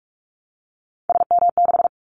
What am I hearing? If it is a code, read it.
Morse code, heard as a keyed tone, about 44 wpm, 700 Hz. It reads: SM6
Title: Callsign Prefixes for Top 25 DXCC - Repeated 3x - 45wpm